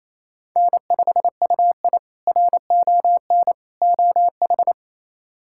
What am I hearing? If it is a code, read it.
Morse code, heard as a keyed tone, about 28 wpm, 700 Hz. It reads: N5UI RON OH
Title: K1USN SST QSO Practice - 28wpm